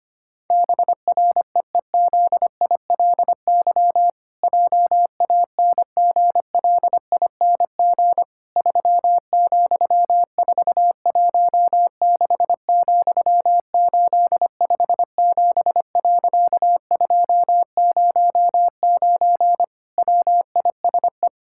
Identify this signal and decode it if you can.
Morse code, heard as a keyed tone, about 25 wpm, 700 Hz. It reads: BREEZILY JANGLING 3,416,857.209 WISE
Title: Pangram Challenge v2 - 25wpm